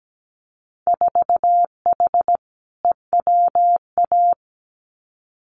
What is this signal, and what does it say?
Morse code, heard as a keyed tone, about 17 wpm, 700 Hz. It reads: 4H EWA
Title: Winter Field Day Exchanges Encore 17wpm